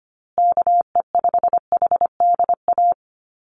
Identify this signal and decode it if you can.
Morse code, heard as a keyed tone, about 25 wpm, 700 Hz. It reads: KE5HDA
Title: USA Call Signs Repeated 2x 25wpm